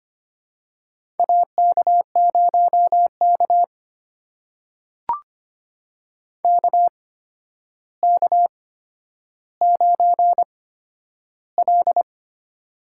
Morse code, heard as a keyed tone, about 26 wpm, spaced as for 11 wpm, 700 Hz. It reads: AK0K KK9L
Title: USA Call Signs - 4 Characters - 8x Character Spacing - 25wpm